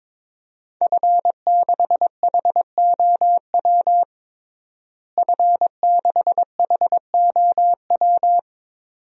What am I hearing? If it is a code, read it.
Morse code, heard as a keyed tone, about 22 wpm, 700 Hz. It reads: F6HOW F6HOW
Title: DXCC-100 Call Signs - 5 Characters - Repeated 2x - 22wpm